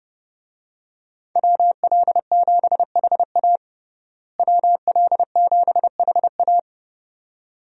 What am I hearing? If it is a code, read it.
Morse code, heard as a keyed tone, about 30 wpm, 700 Hz. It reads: WL7HA WL7HA
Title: USA Call Signs - 5 Characters - Repeated 2x - 30wpm